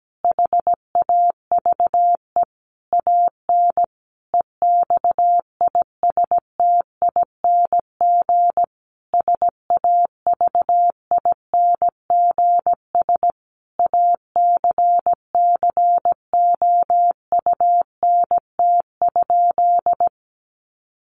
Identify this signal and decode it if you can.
Morse code, heard as a keyed tone, about 17 wpm, 700 Hz. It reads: HAVE AN EXISTING SAVINGS ACCOUNT?
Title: Sentences from Top 1500 Words 17wpm